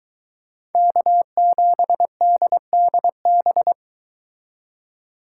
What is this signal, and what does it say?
Morse code, heard as a keyed tone, about 22 wpm, 700 Hz. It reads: K7DDB